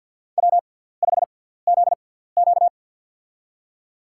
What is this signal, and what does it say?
Morse code, heard as a keyed tone, about 52 wpm, spaced as for 17 wpm, 700 Hz. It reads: W5ZQ
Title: USA Call Signs - 4 Characters - 6x Character Spacing - 50wpm